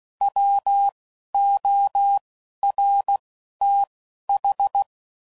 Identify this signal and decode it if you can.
Morse code, heard as a keyed tone, about 16 wpm, 800 Hz. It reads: W O R T H
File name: Prefixes / By First Letter / Ws